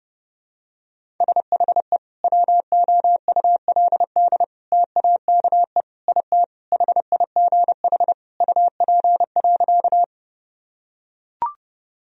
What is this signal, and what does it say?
Morse code, heard as a keyed tone, about 29 wpm, 700 Hz. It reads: SHE WOULD TAKE IT HIGH UP.
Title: Sentences from Top 1000 Words - Review Part 1 - 30wpm